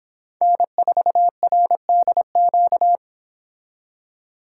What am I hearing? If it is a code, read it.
Morse code, heard as a keyed tone, about 29 wpm, 700 Hz. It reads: N4RDQ